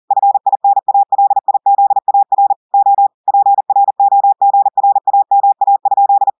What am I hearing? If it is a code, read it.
Morse code, heard as a keyed tone, about 40 wpm, 800 Hz. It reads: FINALIZAR O PROGRAMA?
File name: audios / portugais / portugais texte en clair 2 vitesse 40 mots minutes